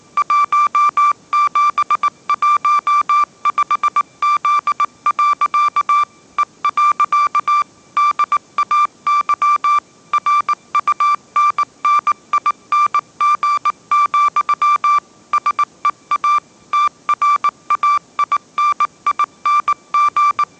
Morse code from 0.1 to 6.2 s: 1715Z.